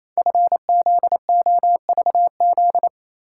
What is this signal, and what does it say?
Morse code, heard as a keyed tone, about 27 wpm, 700 Hz. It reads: FZOVZ
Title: Code Groups - 5 Letters - 28wpm